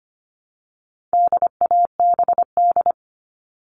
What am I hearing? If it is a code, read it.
Morse code, heard as a keyed tone, about 25 wpm, 700 Hz. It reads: DABD
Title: All Cut Numbers - 4-Digit 25wpm